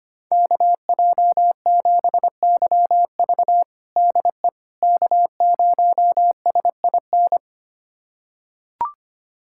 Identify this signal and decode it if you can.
Morse code, heard as a keyed tone, about 25 wpm, 700 Hz. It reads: KJ7YV DE K0SIN
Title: USA Callsign DE Callsign - 5 Characters - 25wpm